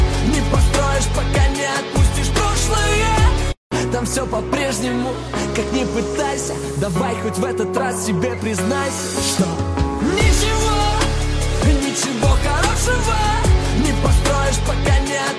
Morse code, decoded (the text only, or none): none